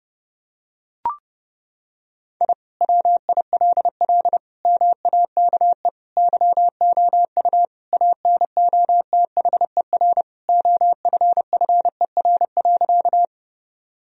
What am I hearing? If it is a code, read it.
Morse code, heard as a keyed tone, about 32 wpm, 700 Hz. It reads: I WILL MAKE YOU ANOTHER OFFER.
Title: Sentences from Top 1000 Words - Review Part 4 - 30wpm